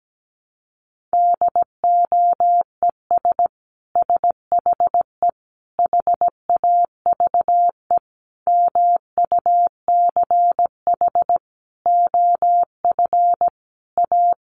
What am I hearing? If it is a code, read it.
Morse code, heard as a keyed tone, about 17 wpm, 700 Hz. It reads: DOES SHE HAVE MUCH OF A
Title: Sentences from Top 1400 Words 17wpm